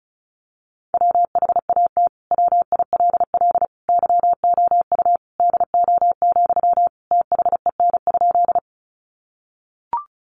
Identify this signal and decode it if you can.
Morse code, heard as a keyed tone, about 34 wpm, 700 Hz. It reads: WHAT WILL YOU DO, THEN?
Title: Sentences from Top 100 Words 35wpm